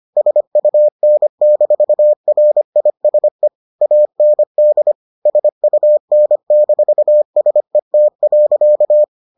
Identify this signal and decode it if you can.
Morse code, heard as a keyed tone, about 25 wpm, 600 Hz. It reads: SUN-RISE AND SUN-SET.